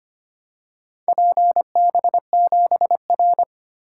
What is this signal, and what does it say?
Morse code, heard as a keyed tone, about 25 wpm, 700 Hz. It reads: PB7R
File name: practice / 25wpm / Four Letter-Number Combinations - 25wpm